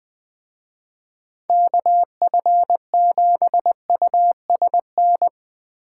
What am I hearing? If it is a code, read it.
Morse code, heard as a keyed tone, about 20 wpm, 700 Hz. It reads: KF7USN
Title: Speed Racing - USA 2x3 Call Signs 17wpm